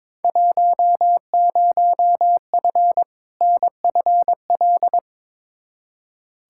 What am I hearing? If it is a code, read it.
Morse code, heard as a keyed tone, about 21 wpm, 700 Hz. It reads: 10F NFL
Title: Field Day Exchanges Encore - 22wpm